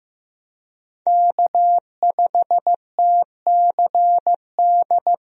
Morse code, heard as a keyed tone, about 15 wpm, 700 Hz. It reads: K5TCD
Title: USA Call Signs - 5 Characters - Repeated 2x - 15wpm